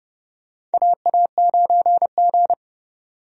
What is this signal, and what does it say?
Morse code, heard as a keyed tone, about 29 wpm, 700 Hz. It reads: AA9G
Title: USA Call Signs - 4 Characters - 30wpm